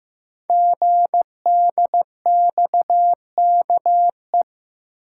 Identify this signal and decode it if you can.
Morse code, heard as a keyed tone, about 15 wpm, 700 Hz. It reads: GDXKE
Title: Code Groups - 5 Letters - 15wpm